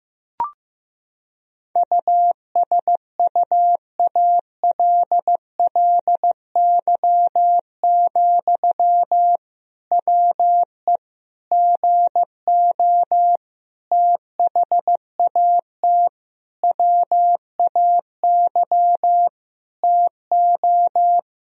Morse code, heard as a keyed tone, about 15 wpm, 700 Hz. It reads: USUALLY, WE GO THAT WAY TO
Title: Sentences from Top 1000 Words - Review Part 3 - 15wpm